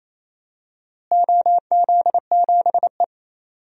Morse code, heard as a keyed tone, about 28 wpm, 700 Hz. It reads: OZ7E
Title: DXCC-100 Call Signs - 4 Characters - Repeated 3x - 28wpm